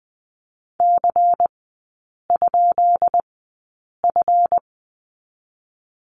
Morse code, heard as a keyed tone, about 20 wpm, spaced as for 8 wpm, 700 Hz. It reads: C?F
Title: 152 Lesson 25e - Letter p - Three Characters 20wpm